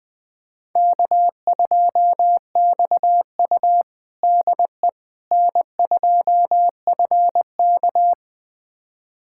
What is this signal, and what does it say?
Morse code, heard as a keyed tone, about 20 wpm, 700 Hz. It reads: K2XU DE N2FK
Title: USA Callsign DE Callsign - 4 Characters - 20wpm